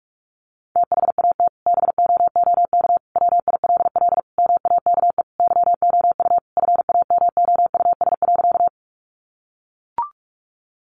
Morse code, heard as a keyed tone, about 44 wpm, 700 Hz. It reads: THAT BOOK WILL MAKE YOU FAMOUS.
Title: Sentences from Top 1500 Words 45wpm